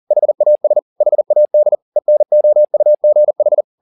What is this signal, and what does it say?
Morse code, heard as a keyed tone, about 40 wpm, 600 Hz. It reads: HAS HAD ENOUGH
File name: the old man and the sea 2 40wpm